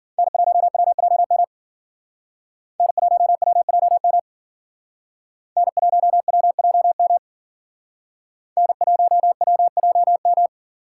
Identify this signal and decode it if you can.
Morse code, heard as a keyed tone, about 49 wpm, 700 Hz. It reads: N1WJM N1WJM N1WJM N1WJM
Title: Speed Racing - USA Call Signs 40wpm